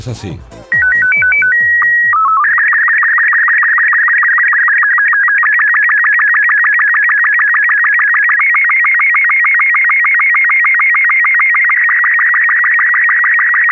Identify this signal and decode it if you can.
SSTV (slow-scan television), Robot 36, with speech.